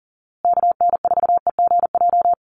Morse code, heard as a keyed tone, about 40 wpm, 700 Hz. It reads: KN4EGJ
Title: USA Call Signs Repeated 3x 40wpm